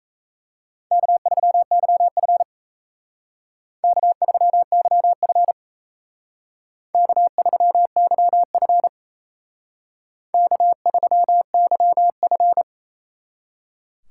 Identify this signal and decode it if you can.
Morse code, heard as a keyed tone, about 34 wpm, 700 Hz. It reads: K3YF K3YF K3YF K3YF
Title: Speed Racing - USA 1x2 Call Signs 28wpm